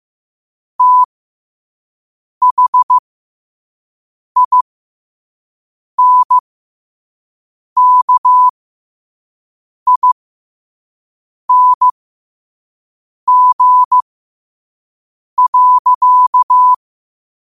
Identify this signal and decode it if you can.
Morse code, heard as a keyed tone, about 15 wpm, spaced as for 5 wpm, 1000 Hz. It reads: THINKING.
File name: qotdfiles / 5wpm